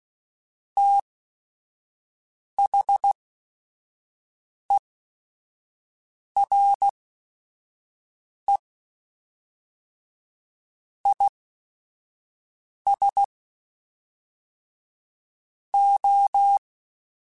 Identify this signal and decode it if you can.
Morse code, heard as a keyed tone, about 16 wpm, spaced as for 5 wpm, 800 Hz. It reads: THERE IS O